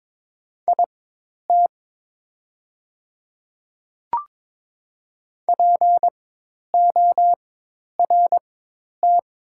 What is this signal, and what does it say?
Morse code, heard as a keyed tone, about 22 wpm, spaced as for 10 wpm, 700 Hz. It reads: IT PORT
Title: 4x Character Spacing - Top 500 Words - 22wpm